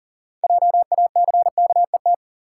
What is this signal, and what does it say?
Morse code, heard as a keyed tone, about 40 wpm, 700 Hz. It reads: JACKET
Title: Six Letter Words - 40wpm